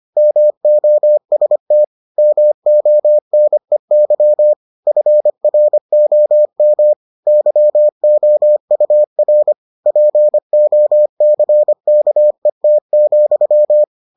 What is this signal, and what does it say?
Morse code, heard as a keyed tone, about 25 wpm, 600 Hz. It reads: MOST MONEY FROM YOUR POCKET,